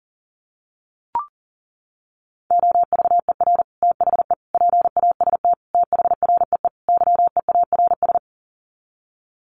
Morse code, heard as a keyed tone, about 40 wpm, 700 Hz. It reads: OVER THE PAST THREE YEARS
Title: Sets of 5 Words Encore2 40wpm